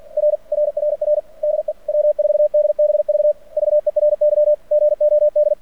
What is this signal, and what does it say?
Morse code, heard as a keyed tone, about 45 wpm, 600 Hz. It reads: R RRR DE W4DBV VERY GOD
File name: archived / CRSnet